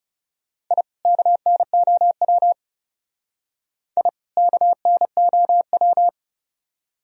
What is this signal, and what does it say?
Morse code, heard as a keyed tone, about 36 wpm, 700 Hz. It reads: I KNOW I KNOW
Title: Speed-Racing - Sets of 2 Words 30wpm